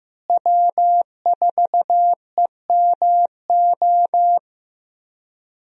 Morse code, heard as a keyed tone, about 16 wpm, 700 Hz. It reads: W4EMO
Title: USA Call Signs - 5 Characters - Repeated 2x - 15wpm